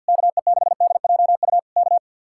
Morse code, heard as a keyed tone, about 62 wpm, 700 Hz. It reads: KE6DJU K
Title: Speed-Racing Common QSO Phrases Encore - 50wpm